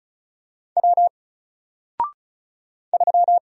Morse code, heard as a keyed tone, about 36 wpm, 700 Hz. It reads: W 3
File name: Single-Letter-Number-35wpm